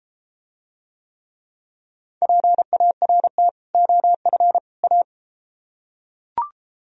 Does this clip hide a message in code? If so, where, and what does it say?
Yes: Morse code from 2.2 to 5.0 s, heard as a keyed tone, about 32 wpm, 700 Hz. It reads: PART OF A